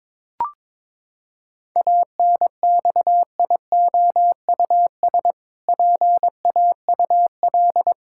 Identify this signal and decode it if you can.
Morse code, heard as a keyed tone, about 22 wpm, 700 Hz. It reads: ANXIOUS PAUL
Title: Pangram Challenge v2 - 22wpm